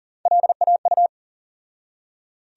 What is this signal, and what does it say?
Morse code, heard as a keyed tone, about 40 wpm, 700 Hz. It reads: LAU
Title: USA Call Sign Postfixes - 3 Letters 40wpm